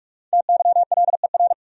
Morse code, heard as a keyed tone, about 46 wpm, 700 Hz. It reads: TYLER
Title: USA Common Names - 45wpm